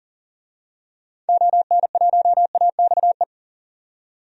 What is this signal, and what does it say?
Morse code, heard as a keyed tone, about 37 wpm, 700 Hz. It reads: ON1AXE
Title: DXCC-100 Call Signs - 6 Characters - 40wpm